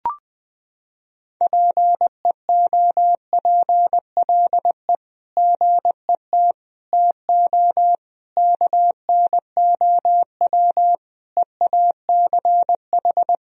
Morse code, heard as a keyed tone, about 20 wpm, 700 Hz. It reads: PEOPLE GET TO KNOW EACH
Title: Sentences from Top 1000 Words - Review Part 1 - 20wpm